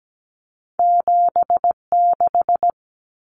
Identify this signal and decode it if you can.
Morse code, heard as a keyed tone, about 17 wpm, 700 Hz. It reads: 76
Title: Numbers - 2-Digit 17wpm